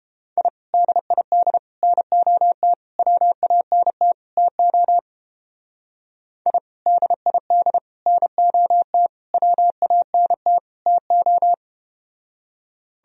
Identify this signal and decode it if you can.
Morse code, heard as a keyed tone, about 32 wpm, 700 Hz. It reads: I DID NOT WANT TO I DID NOT WANT TO